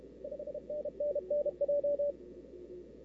Morse code, heard as a keyed tone, about 32 wpm, 590 Hz. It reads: 5NN NJ
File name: CW / 2005-k5tr arrl cw dx -03:00